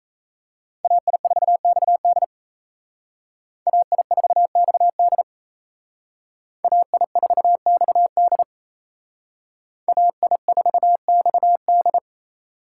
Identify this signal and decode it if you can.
Morse code, heard as a keyed tone, about 32 wpm, 700 Hz. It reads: AI4XD AI4XD AI4XD AI4XD
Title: Speed Racing - USA Call Signs 28wpm